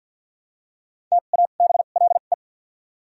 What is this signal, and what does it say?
Morse code, heard as a keyed tone, about 53 wpm, 700 Hz. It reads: T A B L E
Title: 2x Character Spacing - Top 500 Words - 50wpm